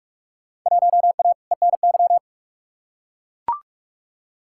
Morse code, heard as a keyed tone, about 42 wpm, 700 Hz. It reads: 1A ENY
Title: Field Day Exchanges Encore - 45wpm